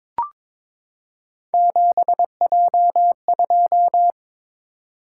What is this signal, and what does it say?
Morse code, heard as a keyed tone, about 22 wpm, 700 Hz. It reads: 7J2
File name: practice / 22wpm / Three Letter-Number Combinations - 22wpm